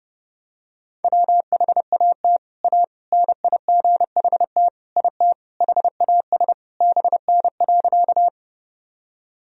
Morse code, heard as a keyed tone, about 30 wpm, 700 Hz. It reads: WHAT A NIGHT IT HAS BN.
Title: Sentences from Top 300 Words and Abbreviations 30wpm